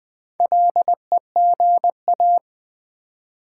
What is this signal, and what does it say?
Morse code, heard as a keyed tone, about 20 wpm, 690 Hz. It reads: LEGA
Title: Build A Word - 2 Letters At A Time - 6-Letter Words - 20wpm